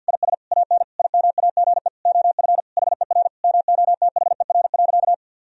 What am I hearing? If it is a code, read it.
Morse code, heard as a keyed tone, about 50 wpm, 700 Hz. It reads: IS AN IMAGE OF HER MOTHER.
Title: Sentences from Top 1000 Words - Review Part 4 - 50wpm